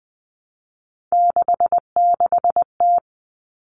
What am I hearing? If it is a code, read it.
Morse code, heard as a keyed tone, about 20 wpm, 700 Hz. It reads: 66T
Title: Common Cut Numbers - 3-Digit 20wpm